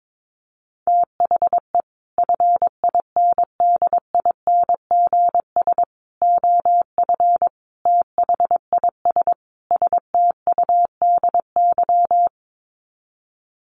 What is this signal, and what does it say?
Morse code, heard as a keyed tone, about 22 wpm, 700 Hz. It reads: THE FINDINGS OF THIS STUDY
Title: Sets of 5 Words Encore2 22wpm